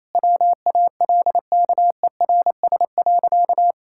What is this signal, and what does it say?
Morse code, heard as a keyed tone, about 28 wpm, 700 Hz. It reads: WALKERS.
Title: Pangram Challenge v2 - 28wpm